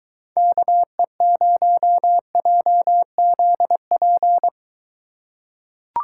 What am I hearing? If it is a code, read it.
Morse code, heard as a keyed tone, about 23 wpm, 700 Hz. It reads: KE0JZP